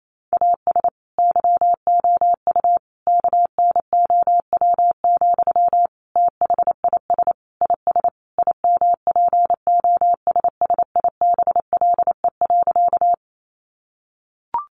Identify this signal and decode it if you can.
Morse code, heard as a keyed tone, about 28 wpm, 700 Hz. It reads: AS YOU KNOW, THIS IS IMPOSSIBLE.
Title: Sentences from Top 1100 Words 28wpm